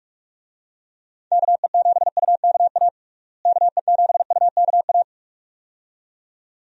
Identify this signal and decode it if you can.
Morse code, heard as a keyed tone, about 41 wpm, 700 Hz. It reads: KE7UKA KE7UKA
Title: USA Call Signs - 6 Characters - Repeated 2x - 45wpm